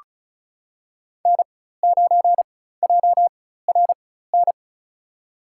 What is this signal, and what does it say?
Morse code, heard as a keyed tone, about 34 wpm, spaced as for 16 wpm, 700 Hz. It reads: N9JRN
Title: USA Call Signs - 5 Characters - 4x Character Spacing - 35wpm